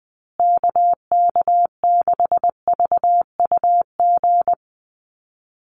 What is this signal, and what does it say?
Morse code, heard as a keyed tone, about 20 wpm, 700 Hz. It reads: KK6VUG
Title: USA Call Signs Repeated 3x 20wpm